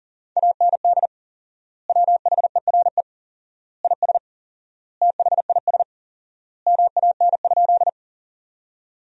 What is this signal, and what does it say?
Morse code, heard as a keyed tone, about 40 wpm, 700 Hz. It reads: AND WHERE IS THIS MAN?
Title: 4x Word-Spacing - Sentences from Top 200 Words - 40wpm